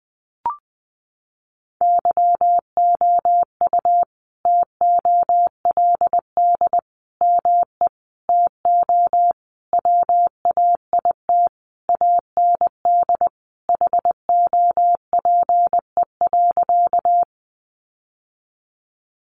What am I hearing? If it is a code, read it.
Morse code, heard as a keyed tone, about 20 wpm, 700 Hz. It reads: YOU TOLD ME TO WAIT AND HOPE.
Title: Sentences from Top 1000 Words 20wpm